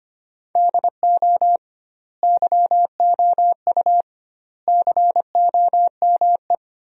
Morse code, heard as a keyed tone, about 25 wpm, 700 Hz. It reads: DO YOU COME
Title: 2x Word-Spacing - Sentences from Top 200 Words - 25wpm